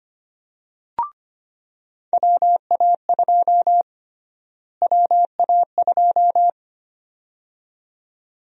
Morse code, heard as a keyed tone, about 25 wpm, 700 Hz. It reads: WA2 WA2
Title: USA Call Sign Prefixes - Repeated 2x - 25wpm